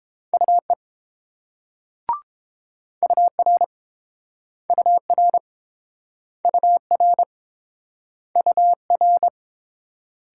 Morse code, heard as a keyed tone, about 30 wpm, 700 Hz. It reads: UE UR UR UR UR